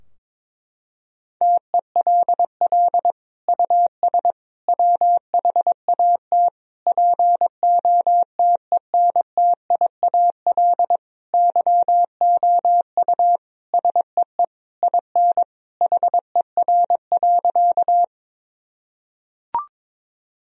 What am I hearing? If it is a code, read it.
Morse code, heard as a keyed tone, about 22 wpm, 700 Hz. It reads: TELL US WHAT POTENTIAL YOU SEE IN HER.
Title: Sentences from Top 1000 Words - Review Part 4 - 22wpm